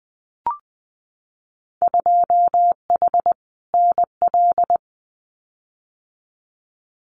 Morse code, heard as a keyed tone, about 19 wpm, 700 Hz. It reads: 2H NL
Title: Winter Field Day Exchanges Encore 20wpm